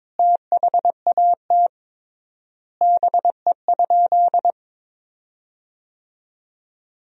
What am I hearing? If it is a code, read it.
Morse code, heard as a keyed tone, about 21 wpm, 700 Hz. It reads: THAT BE?
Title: 3x Word-Spacing - Sentences from Top 100 Words - 22wpm